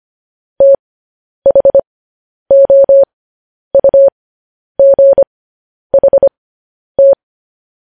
Morse code, heard as a keyed tone, about 25 wpm, spaced as for 10 wpm, 550 Hz. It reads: THOUGHT
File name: podcast / News headlines-1x-25x10